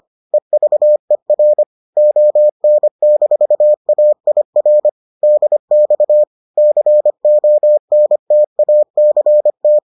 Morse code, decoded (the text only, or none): EVER ON-AIR DX CONTACT